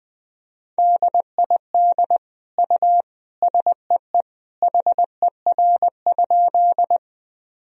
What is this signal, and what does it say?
Morse code, heard as a keyed tone, about 21 wpm, 700 Hz. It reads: DID U SEE HER?
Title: Sentences from Top 100 Words and Abbreviations 20wpm